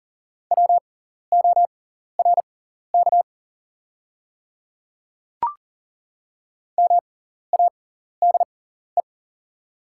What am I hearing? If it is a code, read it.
Morse code, heard as a keyed tone, about 38 wpm, spaced as for 14 wpm, 690 Hz. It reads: WORK MADE
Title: 6x Character Spacing - Top 300 Words - 40wpm